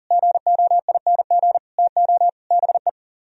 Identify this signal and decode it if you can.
Morse code, heard as a keyed tone, about 51 wpm, 700 Hz. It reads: GOING TO BE